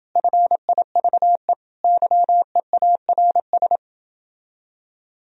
Morse code, heard as a keyed tone, about 30 wpm, 700 Hz. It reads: FIVE YEARS